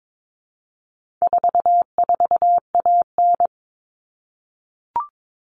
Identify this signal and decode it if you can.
Morse code, heard as a keyed tone, about 22 wpm, 700 Hz. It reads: 44AN
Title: All Cut Numbers - 4-Digit 22wpm